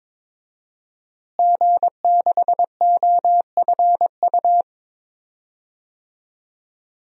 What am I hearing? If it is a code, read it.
Morse code, heard as a keyed tone, about 21 wpm, 700 Hz. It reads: G6OFU
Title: DXCC-100 Call Signs - 5 Characters - Repeated 2x - 22wpm